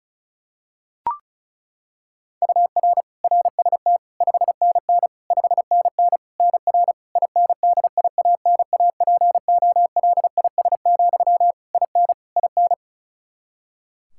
Morse code, heard as a keyed tone, about 35 wpm, 700 Hz. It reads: UR RST 5NN 5NN NR INDIANAPOLIS, IN IN
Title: K1USN SST QSO Practice 35wpm